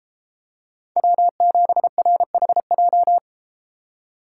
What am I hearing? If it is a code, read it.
Morse code, heard as a keyed tone, about 32 wpm, 700 Hz. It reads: W7RHJ